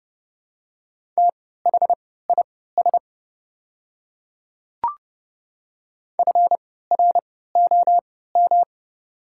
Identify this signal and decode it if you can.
Morse code, heard as a keyed tone, about 31 wpm, spaced as for 17 wpm, 690 Hz. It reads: THIS FROM
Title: 3x Character Spacing - Top 300 Words - 30wpm